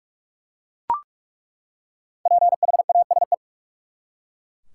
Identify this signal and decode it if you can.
Morse code, heard as a keyed tone, about 42 wpm, 700 Hz. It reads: PHASE
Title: Five Letter Words - 45wpm